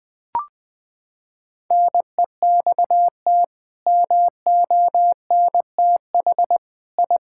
Morse code, heard as a keyed tone, about 20 wpm, 700 Hz. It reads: NEXT MONTH I
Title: Sentences from Top 1000 Words - Review Part 5 - 20wpm